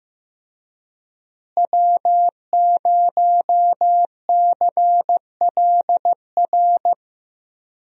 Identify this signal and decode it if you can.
Morse code, heard as a keyed tone, about 15 wpm, 700 Hz. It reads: W0CLR
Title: USA Call Signs - 5 Characters - Repeated 3x - 15wpm